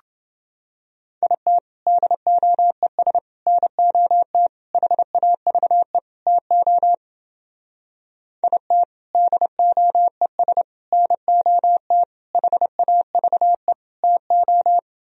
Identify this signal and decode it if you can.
Morse code, heard as a keyed tone, about 30 wpm, 700 Hz. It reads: IT DOES NOT HAVE TO IT DOES NOT HAVE TO